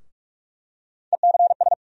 Morse code, heard as a keyed tone, about 46 wpm, 700 Hz. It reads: ECS
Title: USA Call Sign Postfixes - 3 Letters 45wpm